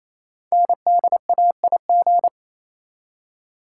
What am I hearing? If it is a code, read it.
Morse code, heard as a keyed tone, about 28 wpm, 700 Hz. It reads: NDAIG
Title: Code Groups - 5 Letters - 28wpm